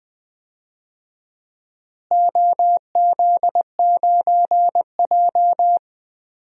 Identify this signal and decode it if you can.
Morse code, heard as a keyed tone, about 20 wpm, 700 Hz. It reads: OZ9J
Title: DXCC-100 Call Signs - 4 Characters - Repeated 2x - 20wpm